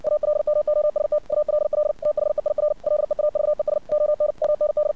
Morse code, heard as a keyed tone, about 60 wpm, 600 Hz. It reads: ALMOST ALL THEIR LEAVES ON MND
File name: archived / CRSnet